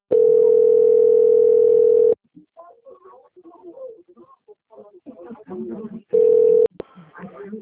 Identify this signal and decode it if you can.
Ringback tone (North America), 440 + 480 Hz, with speech.
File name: out- - - - - .24727